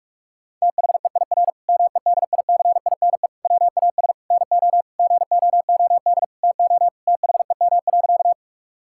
Morse code, heard as a keyed tone, about 46 wpm, 700 Hz. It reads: THEIR MEDICINE WAS NO GOOD TO THEM.
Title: Sentences from Top 1000 Words - Review Part 6 - 45wpm